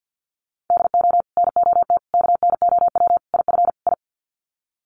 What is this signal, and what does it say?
Morse code, heard as a keyed tone, about 50 wpm, 700 Hz. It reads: DO NOT KNOW IF I
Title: Sets of 5 Words 50wpm